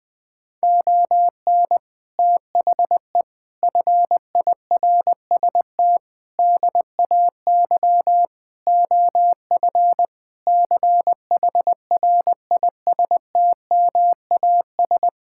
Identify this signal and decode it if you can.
Morse code, heard as a keyed tone, about 20 wpm, 700 Hz. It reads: ON THE FIRST DAY OF CHRISTMAS
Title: Sentences with 8-Word Phrases - Spoken Beginning - 20wpm